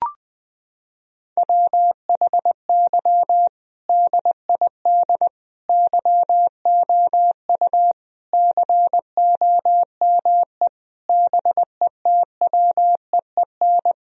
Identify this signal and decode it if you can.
Morse code, heard as a keyed tone, about 21 wpm, 700 Hz. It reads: WHY DID YOU COME BETWEEN
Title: Sentences from Top 1000 Words - Review Part 1 - 20wpm